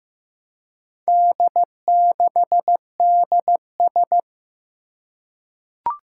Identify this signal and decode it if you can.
Morse code, heard as a keyed tone, about 15 wpm, 700 Hz. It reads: D6DS
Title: Four Challenging Letter-Number Combinations 15wpm